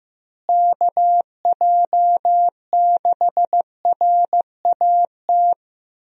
Morse code, heard as a keyed tone, about 15 wpm, 700 Hz. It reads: KJ6RAT
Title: USA Call Signs - 6 Characters - Repeated 3x - 15wpm